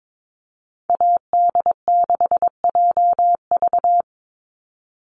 Morse code, heard as a keyed tone, about 22 wpm, 700 Hz. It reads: AD6JV
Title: Call Signs Repeated 3x - 22wpm